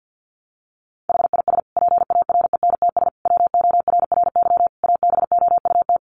Morse code, heard as a keyed tone, about 50 wpm, 700 Hz. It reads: HIS PARENTS WORRY ABOUT
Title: Easy Phrases 50wpm